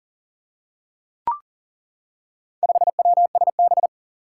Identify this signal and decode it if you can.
Morse code, heard as a keyed tone, about 41 wpm, 700 Hz. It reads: 5WSB
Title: Four Challenging Letter-Number Combinations 40wpm